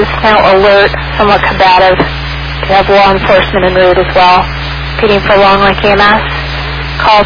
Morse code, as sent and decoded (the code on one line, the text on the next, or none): none